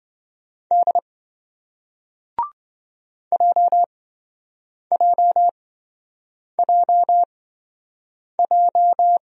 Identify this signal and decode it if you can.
Morse code, heard as a keyed tone, about 28 wpm, spaced as for 7 wpm, 700 Hz. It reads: D JJJJ